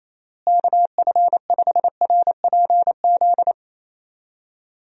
Morse code, heard as a keyed tone, about 28 wpm, 700 Hz. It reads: KF5RPZ
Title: USA Call Signs - 6 Characters - Repeated 2x - 28wpm